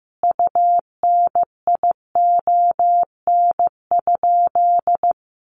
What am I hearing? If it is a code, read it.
Morse code, heard as a keyed tone, about 14 wpm, 700 Hz. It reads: UNION?
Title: Sentences from Top 1400 Words 15wpm